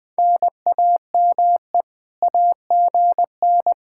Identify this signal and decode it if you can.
Morse code, heard as a keyed tone, about 23 wpm, 700 Hz. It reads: NAME AGN